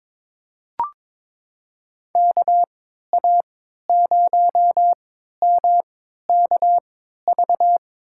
Morse code, heard as a keyed tone, about 22 wpm, spaced as for 13 wpm, 700 Hz. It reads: KA0MKV
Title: USA Call Signs - 6 Characters - 3x Character Spacing - 22wpm